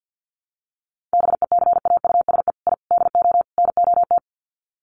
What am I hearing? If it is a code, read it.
Morse code, heard as a keyed tone, about 50 wpm, 700 Hz. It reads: BECAUSE I DO NOT
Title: Sets of 4 Words 50wpm